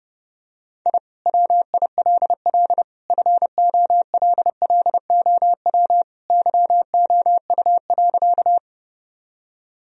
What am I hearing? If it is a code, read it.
Morse code, heard as a keyed tone, about 30 wpm, 700 Hz. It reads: I WILL FOLLOW YOU.
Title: Sentences from Top 1000 Words - Review Part 1 - 30wpm